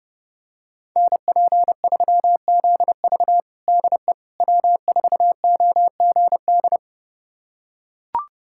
Morse code, heard as a keyed tone, about 32 wpm, 700 Hz. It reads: NP3ZV DE W4OGD
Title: USA Callsign DE Callsign - 5 Characters - 30wpm